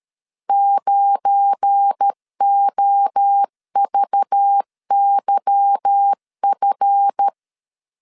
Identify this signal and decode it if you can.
Morse code, heard as a keyed tone, about 13 wpm, 800 Hz. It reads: 9OVYF
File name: LEZIONI / 8 YJ45 / 8 YJ45 12wpm 2151